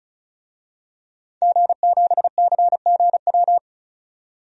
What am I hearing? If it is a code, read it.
Morse code, heard as a keyed tone, about 36 wpm, 700 Hz. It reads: G7CGW
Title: DXCC-100 Call Signs - 5 Characters - 35wpm